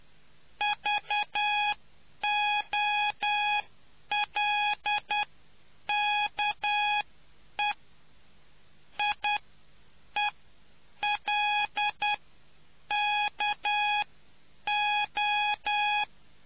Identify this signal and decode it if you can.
Morse code, heard as a keyed tone, about 10 wpm, spaced as for 8 wpm, 830 Hz. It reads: VOLKE IELKO